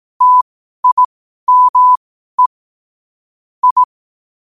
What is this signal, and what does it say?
Morse code, heard as a keyed tone, about 18 wpm, spaced as for 12 wpm, 1000 Hz. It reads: TIME I
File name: qotdfiles / 12wpm